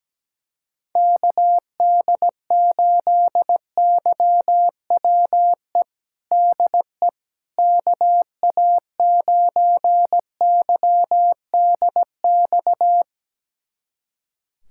Morse code, heard as a keyed tone, about 17 wpm, 700 Hz. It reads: KD8YWE DE KA9YDX
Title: USA Callsign DE Callsign 17wpm